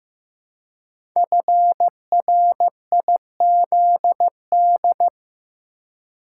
Morse code, heard as a keyed tone, about 15 wpm, 700 Hz. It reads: FRIZD
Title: Code Groups - 5 Letter:Numbers - 15wpm